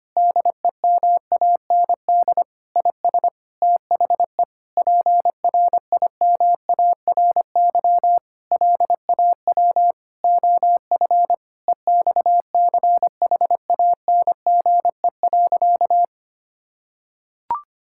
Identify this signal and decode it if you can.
Morse code, heard as a keyed tone, about 25 wpm, 700 Hz. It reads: DEMAND IS THE PRIMARY LAW OF EXCHANGE.
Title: Sentences from Top 1000 Words - Review Part 6 - 25wpm